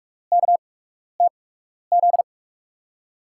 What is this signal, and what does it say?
Morse code, heard as a keyed tone, about 44 wpm, spaced as for 12 wpm, 700 Hz. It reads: KTZ
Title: USA Call Signs - 8x Character Spacing - 45wpm